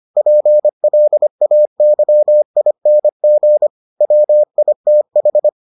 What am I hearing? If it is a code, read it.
Morse code, heard as a keyed tone, about 25 wpm, 600 Hz. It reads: PLAYING WITH